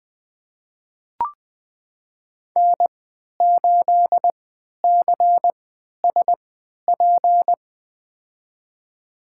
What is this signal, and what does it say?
Morse code, heard as a keyed tone, about 20 wpm, spaced as for 11 wpm, 700 Hz. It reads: N8CSP
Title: USA Call Signs - 3x Character Spacing - 20wpm